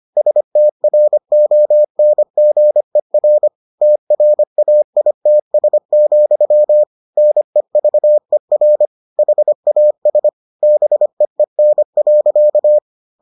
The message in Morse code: STRONGER TRAITS, NEVER HAS BEEN.

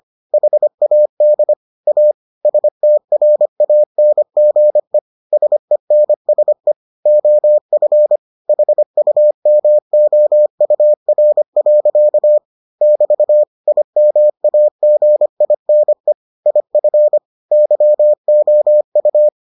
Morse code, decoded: HAD A STRANGE SENSE OF HUMOUR. <BT> IMAGINE IF YOU

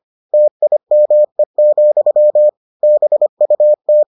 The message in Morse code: TIME, BUT

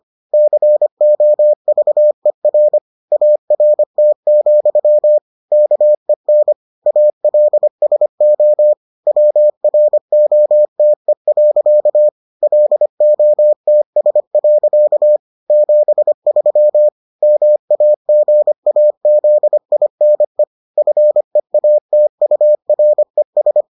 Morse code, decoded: COVER ART, KEN ALSO WROTE. LOTS. 73 MAGAZINE FEATURES

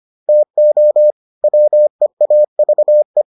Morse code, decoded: TO WEAVE

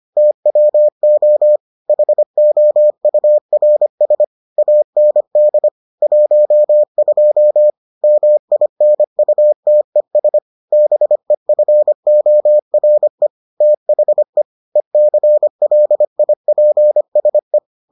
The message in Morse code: TWO HOURS AND 12 MINUTES BEFORE THE ECLIPSE

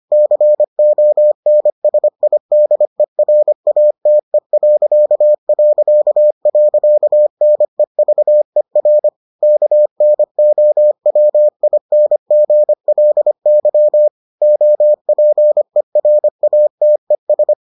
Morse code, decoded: CONSIDERATE...NEVER KNOWINGLY OPERATES

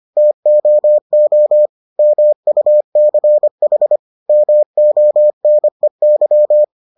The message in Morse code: TOO MUCH MONEY

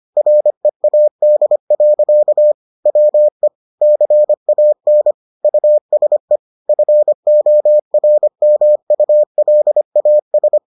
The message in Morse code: READ. WE CAN USE FORMULAS

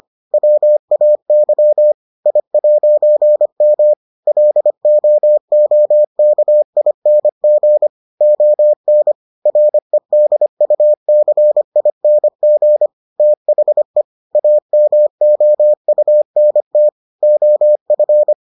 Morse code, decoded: WAY I'M LOOKING ON REDUCING THE AMOUNT OF

